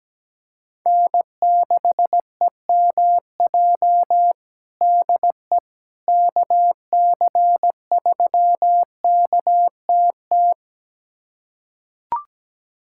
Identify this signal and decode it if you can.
Morse code, heard as a keyed tone, about 17 wpm, 700 Hz. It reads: N6EMJ DE KC3KTT
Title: USA Callsign DE Callsign 17wpm